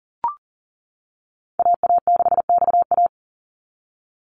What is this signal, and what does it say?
Morse code, heard as a keyed tone, about 40 wpm, 700 Hz. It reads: AA6XA
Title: NA SOTA Activator Callsigns 40wpm